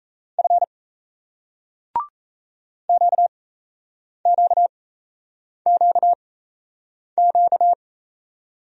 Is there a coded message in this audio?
Yes: Morse code from 0.4 to 7.7 s, heard as a keyed tone, about 38 wpm, spaced as for 8 wpm, 700 Hz. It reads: F QQQQ